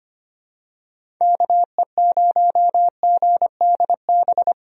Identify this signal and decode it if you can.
Morse code, heard as a keyed tone, about 25 wpm, 700 Hz. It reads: KE0GDB
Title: USA Call Signs - 6 Characters - 25wpm